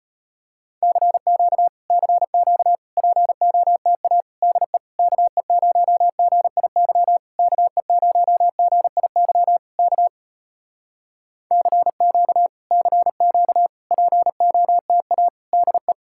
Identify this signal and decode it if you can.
Morse code, heard as a keyed tone, about 33 wpm, 700 Hz. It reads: CQ CQ POTA DE KE0GIY KE0GIY K CQ CQ POTA DE